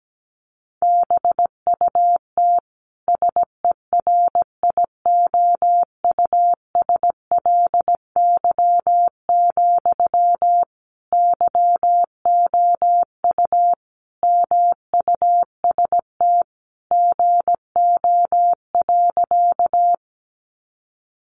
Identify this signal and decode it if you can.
Morse code, heard as a keyed tone, about 17 wpm, 700 Hz. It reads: BUT SERIOUSLY, YOU MUST GO.
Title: Sentences from Top 1200 Words 17wpm